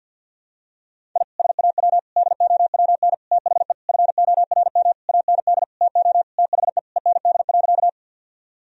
Morse code, heard as a keyed tone, about 50 wpm, 700 Hz. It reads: I SAW DOWN THE FORM AND TO THE END.
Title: Sentences from Top 1000 Words - Review Part 1 - 50wpm